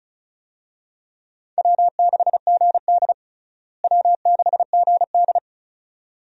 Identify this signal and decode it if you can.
Morse code, heard as a keyed tone, about 34 wpm, 700 Hz. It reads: W6GD W6GD
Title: USA Call Signs - 4 Characters - Repeated 2x - 35wpm